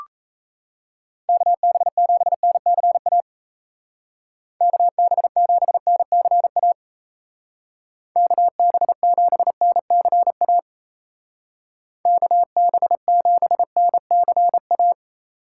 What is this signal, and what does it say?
Morse code, heard as a keyed tone, about 36 wpm, 700 Hz. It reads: KB7NCA KB7NCA KB7NCA KB7NCA